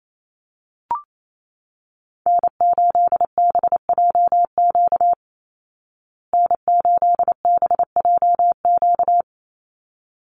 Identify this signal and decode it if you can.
Morse code, heard as a keyed tone, about 28 wpm, 700 Hz. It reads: N8BJQ N8BJQ
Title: Call Signs Repeated 3x - 28wpm